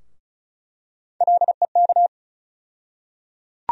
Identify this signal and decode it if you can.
Morse code, heard as a keyed tone, about 36 wpm, 700 Hz. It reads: LEK
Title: USA Call Sign Postfixes - 3 Letters 35wpm